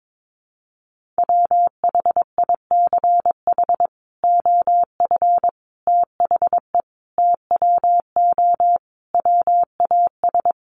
Morse code, heard as a keyed tone, about 22 wpm, 700 Hz. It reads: WHICH OF THE TWO WAS
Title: Sentences from Top 1300 Words 22wpm